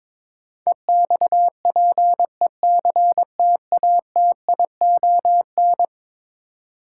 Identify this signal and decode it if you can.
Morse code, heard as a keyed tone, about 22 wpm, 700 Hz. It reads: EXPECTATION
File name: practice / 22wpm / Eleven Letter Words - 22wpm